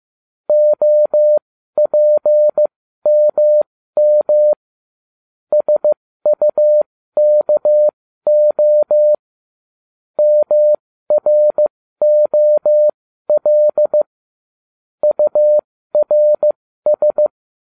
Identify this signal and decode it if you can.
Morse code, heard as a keyed tone, about 15 wpm, spaced as for 12 wpm, 600 Hz. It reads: OPMM SUKO MROL URS